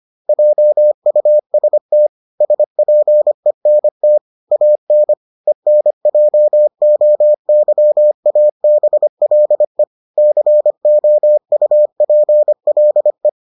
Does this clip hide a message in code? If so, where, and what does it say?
Yes: Morse code from 0.3 to 13.3 s, heard as a keyed tone, about 25 wpm, 600 Hz. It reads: JUST SPENT AN ENJOYABLE COUPLE